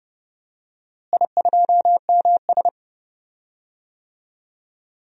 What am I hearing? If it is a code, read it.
Morse code, heard as a keyed tone, about 31 wpm, 700 Hz. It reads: I2MS
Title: DXCC-100 Call Signs - 4 Characters - Repeated 3x - 30wpm